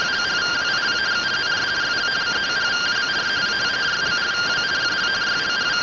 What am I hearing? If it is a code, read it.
Two-tone teleprinter signal, about 170 Hz shift.